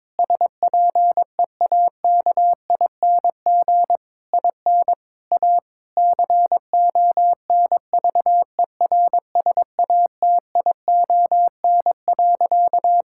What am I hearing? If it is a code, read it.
Morse code, heard as a keyed tone, about 22 wpm, 700 Hz. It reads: SPEAKING IN A CONVERSATION.
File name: Difficult Sentences with Hints - 22wpm